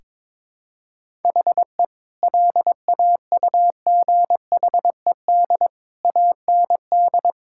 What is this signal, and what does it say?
Morse code, heard as a keyed tone, about 22 wpm, 700 Hz. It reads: HE LAUGHED AND
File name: practice / 22wpm / Sentences with 2-Word Phrases - 22wpm